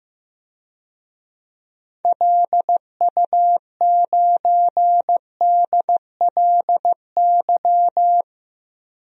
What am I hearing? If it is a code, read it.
Morse code, heard as a keyed tone, about 15 wpm, 700 Hz. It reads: LU9DLY
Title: DXCC-100 Call Signs - 6 Characters - 15wpm